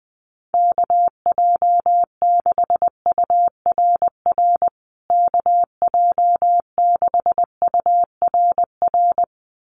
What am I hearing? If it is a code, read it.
Morse code, heard as a keyed tone, about 20 wpm, 700 Hz. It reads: KJ6URR KJ6URR
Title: USA Call Signs Repeated 2x 20wpm